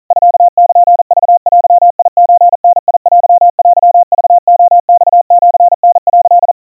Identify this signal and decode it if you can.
Morse code (Japanese Wabun code), heard as a keyed tone, about 40 wpm, 700 Hz. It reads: ミルクデソダテテクレマシタ」
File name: kawa / Morse sample / kamyu 150 cpm 700 Hz Crispy